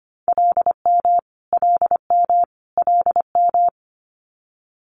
Morse code, heard as a keyed tone, about 25 wpm, 700 Hz. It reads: LM LM LM
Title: Multi-letter Phonemes Repeated 3x 25wpm